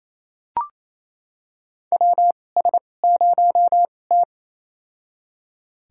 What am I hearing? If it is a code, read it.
Morse code, heard as a keyed tone, about 28 wpm, spaced as for 20 wpm, 700 Hz. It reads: WS0T
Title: USA Call Signs - 2x Character Spacing - 28wpm